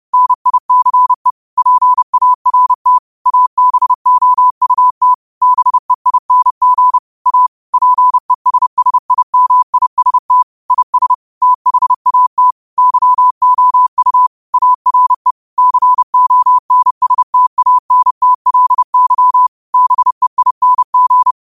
Morse code, heard as a keyed tone, about 30 wpm, 1000 Hz. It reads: NICE PART ABOUT BEING A PESSIMIST IS THAT YOU ARE CONSTANTLY BEING
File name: qotdfiles / 30wpm